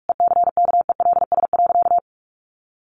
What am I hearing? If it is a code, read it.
Morse code, heard as a keyed tone, about 46 wpm, 700 Hz. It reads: ECKELS.
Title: A Sound Of Thunder 45wpm